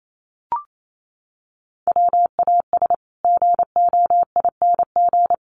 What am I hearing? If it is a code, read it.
Morse code, heard as a keyed tone, about 28 wpm, 700 Hz. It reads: WAS GOING
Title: Sets of 5 Words Encore 28wpm